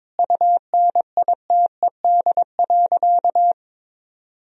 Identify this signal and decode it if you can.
Morse code, heard as a keyed tone, about 22 wpm, 700 Hz. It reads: UNITED.
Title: Sentences from Top 1000 Words - Review Part 4 - 22wpm